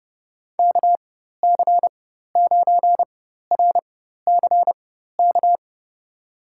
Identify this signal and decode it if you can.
Morse code, heard as a keyed tone, about 30 wpm, spaced as for 14 wpm, 700 Hz. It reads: KC9RCK
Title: USA Call Signs - 6 Characters - 4x Character Spacing - 30wpm